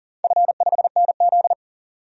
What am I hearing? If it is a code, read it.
Morse code, heard as a keyed tone, about 40 wpm, 700 Hz. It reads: F5NZ
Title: DXCC-100 Call Signs - 4 Characters - 40wpm